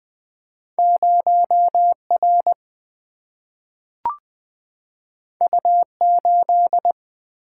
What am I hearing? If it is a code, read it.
Morse code, heard as a keyed tone, about 20 wpm, 700 Hz. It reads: 0 R U 8
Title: Single-Letter Single-Number 20wpm